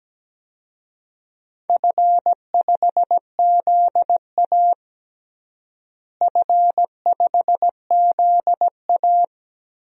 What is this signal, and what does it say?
Morse code, heard as a keyed tone, about 17 wpm, 700 Hz. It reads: F5ZA F5ZA
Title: DXCC-100 Call Signs - 4 Characters - Repeated 3x - 17wpm